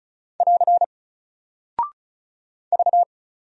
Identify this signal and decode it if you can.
Morse code, heard as a keyed tone, about 36 wpm, 700 Hz. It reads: <AR> V
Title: Single Character (Letter:Number:Punctuation:Prosign) Alt 35wpm